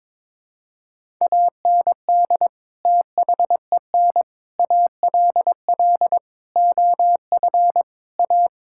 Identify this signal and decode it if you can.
Morse code, heard as a keyed tone, about 22 wpm, 700 Hz. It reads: AND THEN ALL OF A
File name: practice / 22wpm / Sets of 6 Words - 22wpm